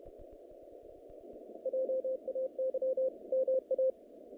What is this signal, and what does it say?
Morse code, heard as a keyed tone, about 30 wpm, 520 Hz. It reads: JAY MA
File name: contests / naqp-cw-k5tr-op-k5pi- -2200